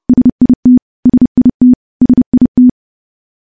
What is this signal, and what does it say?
Morse code, heard as a keyed tone, about 29 wpm, 250 Hz. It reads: SIT SIT SIT
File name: morse / morse-training-30WPM